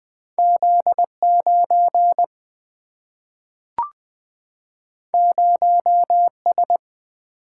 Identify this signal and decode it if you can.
Morse code, heard as a keyed tone, about 20 wpm, 700 Hz. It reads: Z9 0S
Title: Single-Letter Single-Number 20wpm